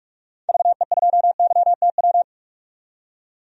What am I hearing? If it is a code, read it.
Morse code, heard as a keyed tone, about 46 wpm, 700 Hz. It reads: VE2YTW
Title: DXCC-100 Call Signs - 6 Characters - 45wpm